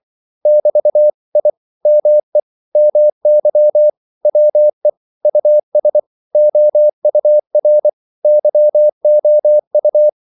Morse code, heard as a keyed tone, about 24 wpm, 600 Hz. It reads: <BT> I ME MY WE US OUR YOU
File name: CW / Practice / 100MostCommonEnglishWords24